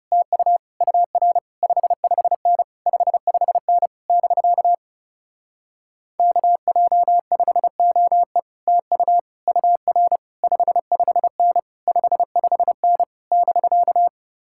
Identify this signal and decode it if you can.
Morse code, heard as a keyed tone, about 36 wpm, 700 Hz. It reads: TU UR 55N 55N <BK> KJ5OE TU UR 55N 55N <BK>
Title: Speed-Racing Common QSO Phrases - 30wpm